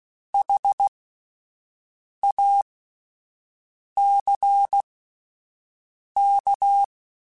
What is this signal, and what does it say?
Morse code, heard as a keyed tone, about 16 wpm, spaced as for 5 wpm, 800 Hz. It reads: HACK
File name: Alphabetical / Alphabetical mp3s / Hs @5